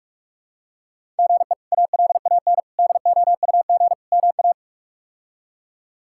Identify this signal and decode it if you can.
Morse code, heard as a keyed tone, about 46 wpm, 700 Hz. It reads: GE ALAN DOUG MA
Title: K1USN SST QSO Practice - 45wpm